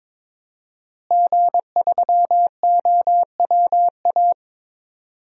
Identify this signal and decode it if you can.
Morse code, heard as a keyed tone, about 22 wpm, 700 Hz. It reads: G3OWA
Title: DXCC-100 Call Signs - 5 Characters - Repeated 3x - 22wpm